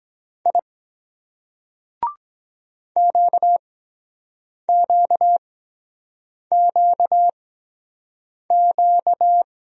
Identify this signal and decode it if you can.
Morse code, heard as a keyed tone, about 23 wpm, spaced as for 7 wpm, 700 Hz. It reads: I QQQQ